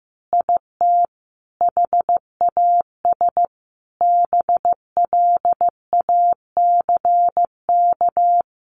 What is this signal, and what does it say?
Morse code, heard as a keyed tone, about 15 wpm, 700 Hz. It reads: IT HAS BLACK
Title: Easy Phrases 15wpm